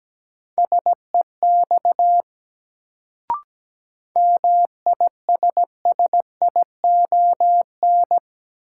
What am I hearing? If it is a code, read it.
Morse code, heard as a keyed tone, about 17 wpm, 700 Hz. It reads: SEX MISSION
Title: Sentences from Top 1600 Words - New Word Drill - 17wpm